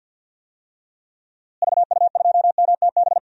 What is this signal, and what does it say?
Morse code, heard as a keyed tone, about 47 wpm, 700 Hz. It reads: VU2MTB
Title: DXCC-100 Call Signs - 6 Characters - Repeated 3x - 50wpm